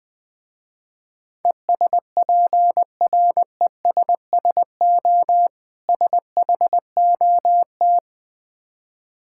Morse code, heard as a keyed tone, about 20 wpm, 700 Hz. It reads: ESPRESSO SHOT
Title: Difficult Words with Hints - 20wpm